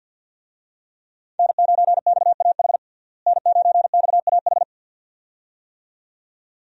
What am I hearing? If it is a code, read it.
Morse code, heard as a keyed tone, about 51 wpm, 700 Hz. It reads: N9XAH N9XAH
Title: USA Call Signs - 5 Characters - Repeated 2x - 50wpm